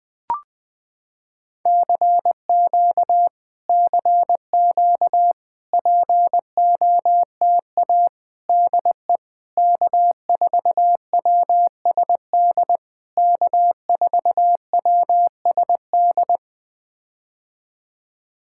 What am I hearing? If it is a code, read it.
Morse code, heard as a keyed tone, about 20 wpm, 700 Hz. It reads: CQ CQ POTA DE K4WSD K4WSD
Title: POTA QSO Practice - 20wpm